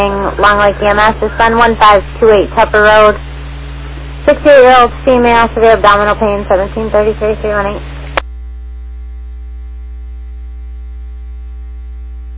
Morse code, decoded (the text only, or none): none